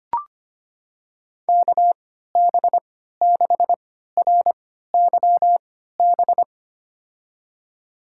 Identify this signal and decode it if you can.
Morse code, heard as a keyed tone, about 25 wpm, spaced as for 14 wpm, 700 Hz. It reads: KB6RYB
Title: USA Call Signs - 6 Characters - 3x Character Spacing - 25wpm